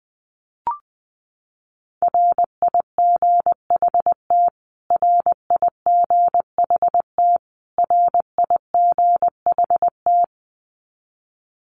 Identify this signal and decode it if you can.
Morse code, heard as a keyed tone, about 20 wpm, 700 Hz. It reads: RIGHT RIGHT RIGHT
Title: Top 200 Words Repeated 20wpm